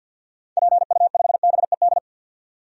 Morse code, heard as a keyed tone, about 50 wpm, 700 Hz. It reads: PU5BED
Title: DXCC-100 Call Signs - 6 Characters - 50wpm